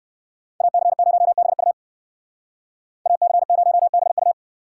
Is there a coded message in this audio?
Yes: Morse code from 0.6 to 4.3 s, heard as a keyed tone, about 59 wpm, 700 Hz. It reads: AC0BU AC0BU